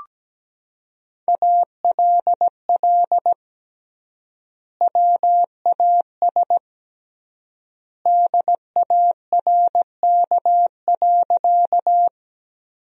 Morse code, heard as a keyed tone, about 17 wpm, 700 Hz. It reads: ALL WAS DARK.
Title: 3x Word-Spacing - Sentences from Top 500 Words - 17wpm